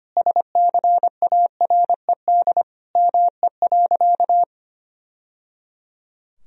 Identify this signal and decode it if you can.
Morse code, heard as a keyed tone, about 25 wpm, 700 Hz. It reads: SCARED ME.
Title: Sentences from Top 1000 Words - Review Part 4 - 25wpm